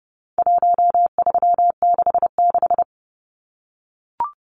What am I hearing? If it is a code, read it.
Morse code, heard as a keyed tone, about 29 wpm, 700 Hz. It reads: 1366
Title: Numbers - Contest Style - 4-Digit 30wpm